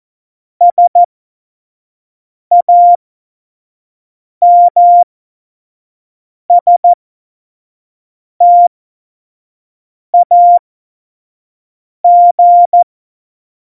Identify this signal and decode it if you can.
Morse code, heard as a keyed tone, about 14 wpm, spaced as for 5 wpm, 700 Hz. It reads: SAMSTAG
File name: Morsekurs / Witze / Witz 14 Juli 2020, 25 F70